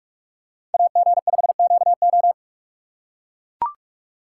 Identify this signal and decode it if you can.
Morse code, heard as a keyed tone, about 43 wpm, 700 Hz. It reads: AG5QO